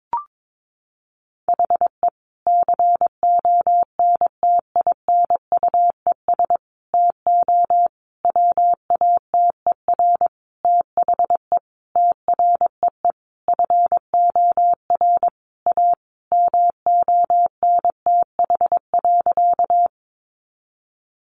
Morse code, heard as a keyed tone, about 22 wpm, 700 Hz. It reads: HE CONTINUES TO WATER THE TREE FOR A MONTH.
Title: Easy Phrases 22wpm